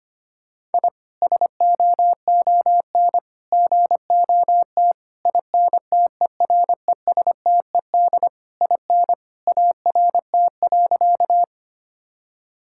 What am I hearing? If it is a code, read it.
Morse code, heard as a keyed tone, about 25 wpm, 700 Hz. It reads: I SOON GOT INTERESTED IN ART.
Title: Sentences from Top 1000 Words - Review Part 6 - 25wpm